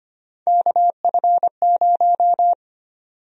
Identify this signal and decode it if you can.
Morse code, heard as a keyed tone, about 25 wpm, 700 Hz. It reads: KF0
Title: USA Call Sign Prefixes - 25wpm